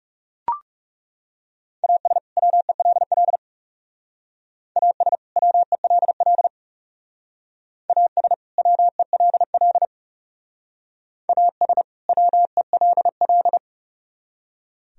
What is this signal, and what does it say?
Morse code, heard as a keyed tone, about 38 wpm, 700 Hz. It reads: AS WELL AS WELL AS WELL AS WELL